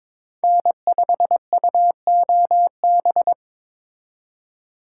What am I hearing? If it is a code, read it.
Morse code, heard as a keyed tone, about 26 wpm, 700 Hz. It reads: N5UOB
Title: Speed Racing - USA 1x3 Call Signs 22wpm